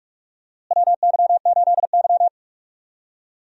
Morse code, heard as a keyed tone, about 41 wpm, 700 Hz. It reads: WY8Y
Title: USA Call Signs - 4 Characters - Repeated 3x - 45wpm